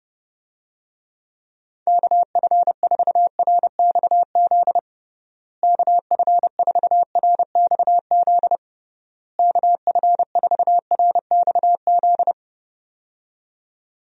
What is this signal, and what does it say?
Morse code, heard as a keyed tone, about 30 wpm, 700 Hz. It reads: KF4RXZ KF4RXZ KF4RXZ
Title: USA Call Signs - 6 Characters - Repeated 3x - 30wpm